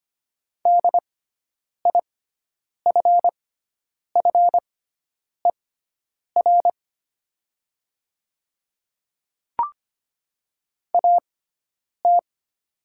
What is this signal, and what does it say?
Morse code, heard as a keyed tone, about 25 wpm, spaced as for 9 wpm, 700 Hz. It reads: DIFFER AT
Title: 6x Character Spacing - Top 200 Words - 25wpm